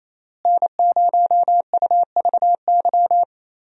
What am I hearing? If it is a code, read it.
Morse code, heard as a keyed tone, about 28 wpm, 700 Hz. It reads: N0UVY
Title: USA Call Signs - 5 Characters - Repeated 2x - 28wpm